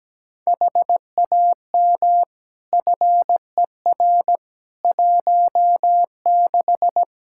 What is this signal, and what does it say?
Morse code, heard as a keyed tone, about 17 wpm, 700 Hz. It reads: HAM FER 16
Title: Practice QSO - Exchange 3 - More Info - 17wpm